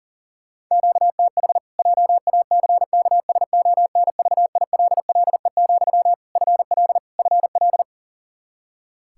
Morse code, heard as a keyed tone, about 40 wpm, 700 Hz. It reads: QTH JACKSONVILLE, FL FL
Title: Practice QSO - Exchange 1 - Beginning - 40wpm